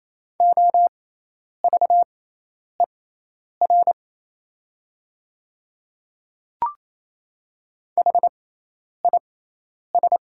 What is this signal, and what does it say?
Morse code, heard as a keyed tone, about 28 wpm, spaced as for 10 wpm, 700 Hz. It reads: OVER HIS
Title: 6x Character Spacing - Top 100 Words - 28wpm